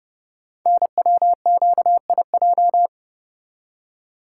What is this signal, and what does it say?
Morse code, heard as a keyed tone, about 29 wpm, 700 Hz. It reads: NWQIJ
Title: Code Groups - 5 Letter:Numbers - 30wpm